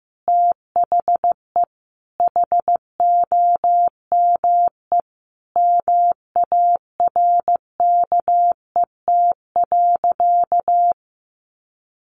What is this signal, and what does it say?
Morse code, heard as a keyed tone, about 14 wpm, 700 Hz. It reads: THE HOME MARKET.
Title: Sentences from Top 1400 Words 15wpm